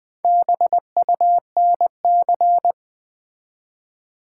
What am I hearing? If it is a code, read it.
Morse code, heard as a keyed tone, about 20 wpm, 700 Hz. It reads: BUNC
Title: Build A Word - 1 Letter At A Time - 3 to 5-Letter Words - 20wpm